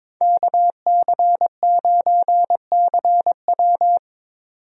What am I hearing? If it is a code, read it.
Morse code, heard as a keyed tone, about 22 wpm, 700 Hz. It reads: KC9CW
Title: USA Call Signs - 5 Characters - 22wpm